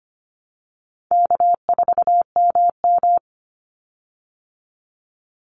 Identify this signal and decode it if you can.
Morse code, heard as a keyed tone, about 25 wpm, 700 Hz. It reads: K4MM
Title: Call Signs Repeated 3x - 25wpm